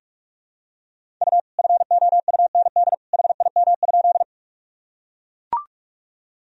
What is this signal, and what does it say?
Morse code, heard as a keyed tone, about 41 wpm, 700 Hz. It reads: U FOUND HIM?
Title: Sentences from Top 300 Words and Abbreviations 45wpm